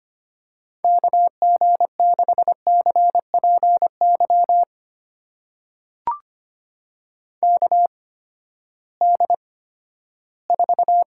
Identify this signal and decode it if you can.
Morse code, heard as a keyed tone, about 25 wpm, spaced as for 14 wpm, 700 Hz. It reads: KG6CPY KD4
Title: USA Call Signs - 6 Characters - 8x Character Spacing - 25wpm